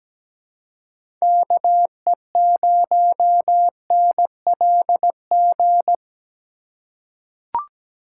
Morse code, heard as a keyed tone, about 17 wpm, 700 Hz. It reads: KE0NLG
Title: USA Call Signs - 6 Characters - 4x Character Spacing - 17wpm